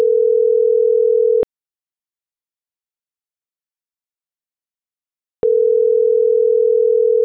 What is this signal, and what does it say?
Ringback tone (North America), 440 + 480 Hz.